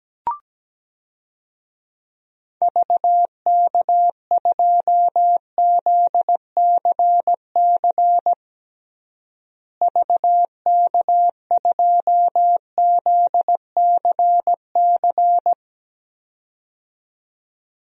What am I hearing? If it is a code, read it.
Morse code, heard as a keyed tone, about 17 wpm, 700 Hz. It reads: VK2ZCC VK2ZCC
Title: DXCC-100 Call Signs - 6 Characters - Repeated 2x - 17wpm